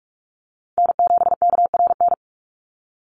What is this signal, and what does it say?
Morse code, heard as a keyed tone, about 44 wpm, 700 Hz. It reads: N7KRN
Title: NA SOTA Chaser Callsigns 45wpm